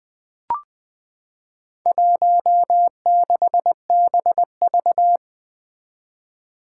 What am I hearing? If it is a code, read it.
Morse code, heard as a keyed tone, about 21 wpm, 700 Hz. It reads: 16BV
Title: Four Challenging Letter-Number Combinations 20wpm